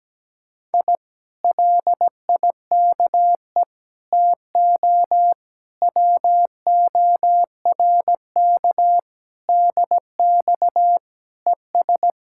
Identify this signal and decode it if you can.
Morse code, heard as a keyed tone, about 17 wpm, 700 Hz. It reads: I LIKE TO WORK DX ES
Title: Speed-Racing Common QSO Phrases - 17wpm